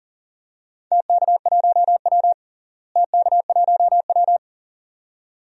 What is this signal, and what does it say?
Morse code, heard as a keyed tone, about 38 wpm, 700 Hz. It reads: TK1W TK1W
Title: DXCC-100 Call Signs - 4 Characters - Repeated 2x - 40wpm